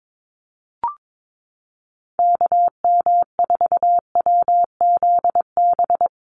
Morse code, heard as a keyed tone, about 22 wpm, 700 Hz. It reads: KM4WZB
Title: USA Call Signs Repeated 3x 22wpm